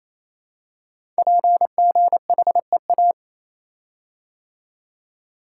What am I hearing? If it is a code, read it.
Morse code, heard as a keyed tone, about 27 wpm, 700 Hz. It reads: PGHEA
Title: Code Groups - 5 Letters - 28wpm